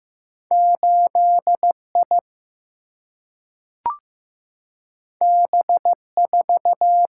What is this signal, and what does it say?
Morse code, heard as a keyed tone, about 15 wpm, 700 Hz. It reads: 8 I B 4
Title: Single-Letter Single-Number 15wpm